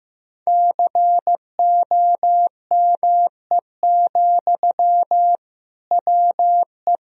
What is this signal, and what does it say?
Morse code, heard as a keyed tone, about 16 wpm, 700 Hz. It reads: COME, WE
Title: Sentences from Top 1000 Words - Review Part 1 - 15wpm